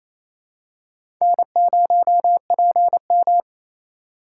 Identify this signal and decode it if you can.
Morse code, heard as a keyed tone, about 28 wpm, 700 Hz. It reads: N0PM
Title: USA Call Signs - 4x Character Spacing - 28wpm